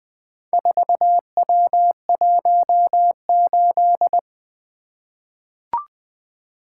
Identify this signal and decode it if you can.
Morse code, heard as a keyed tone, about 20 wpm, 700 Hz. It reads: 4W18
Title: Four Letter-Number Combinations - 20wpm